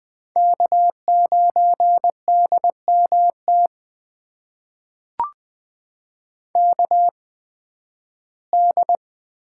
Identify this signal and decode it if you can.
Morse code, heard as a keyed tone, about 20 wpm, spaced as for 13 wpm, 700 Hz. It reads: K9DMT KD
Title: USA Call Signs - 5 Characters - 8x Character Spacing - 20wpm